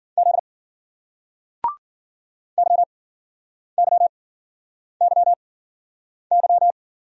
Morse code, heard as a keyed tone, about 55 wpm, 700 Hz. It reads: Z Y Y Y Y